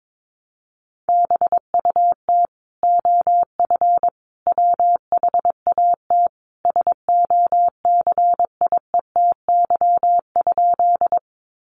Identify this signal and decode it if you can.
Morse code, heard as a keyed tone, about 22 wpm, 700 Hz. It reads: BUT OF WHAT SOCIETY?
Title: Sentences from Top 700 Words 22wpm